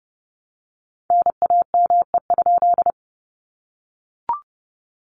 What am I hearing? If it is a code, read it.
Morse code, heard as a keyed tone, about 30 wpm, 700 Hz. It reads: NAME?
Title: Common QSO Phrases Encore 30wpm